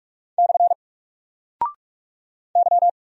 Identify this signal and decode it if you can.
Morse code, heard as a keyed tone, about 46 wpm, 700 Hz. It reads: / Y
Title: Single Character (Letter:Number:Punctuation:Prosign) 45wpm